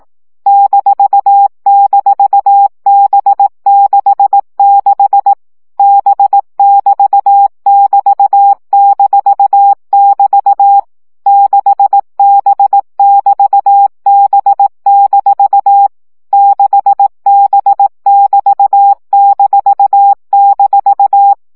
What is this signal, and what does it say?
Morse code, heard as a keyed tone, about 18 wpm, 800 Hz. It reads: --B66 B<BT><BT>-<BT> 6B<BT>B- 6B<BT>--
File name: CW gyakorlas b 6 egyenlo minusz 15 baud